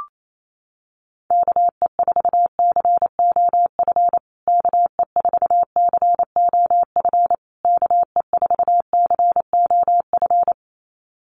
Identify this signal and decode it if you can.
Morse code, heard as a keyed tone, about 28 wpm, 700 Hz. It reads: KE4COF KE4COF KE4COF
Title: USA Call Signs Repeated 3x 28wpm